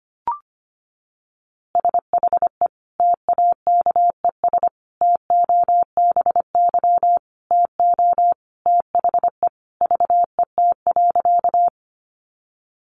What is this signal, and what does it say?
Morse code, heard as a keyed tone, about 25 wpm, 700 Hz. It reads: SHE TAKES TOBY TO THE VET.
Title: Easy Phrases 25wpm